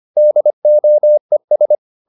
Morse code, heard as a keyed tone, about 25 wpm, 600 Hz. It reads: DOES